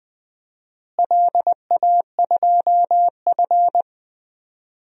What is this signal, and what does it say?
Morse code, heard as a keyed tone, about 21 wpm, 700 Hz. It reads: LA2F
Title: DXCC-100 Call Signs - 4 Characters - Repeated 2x - 20wpm